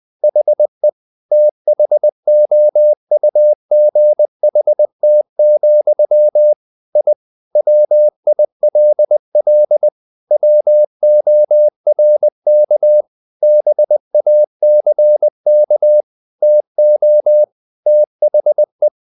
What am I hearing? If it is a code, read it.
Morse code, heard as a keyed tone, about 20 wpm, 600 Hz. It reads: HE THOUGHT, I WILL WORK BACK TO THE
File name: the old man and the sea / the old man and the sea part 1 20wpm